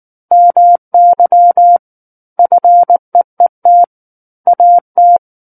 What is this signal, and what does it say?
Morse code, heard as a keyed tone, about 19 wpm, 700 Hz. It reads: MY FEET AT
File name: Book One / 10-Chapter 10 - 19 WPM